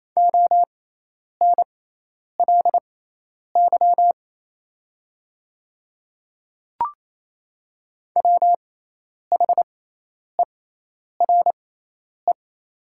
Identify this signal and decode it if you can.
Morse code, heard as a keyed tone, about 28 wpm, spaced as for 10 wpm, 700 Hz. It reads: ONLY WHERE
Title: 6x Character Spacing - Top 200 Words - 28wpm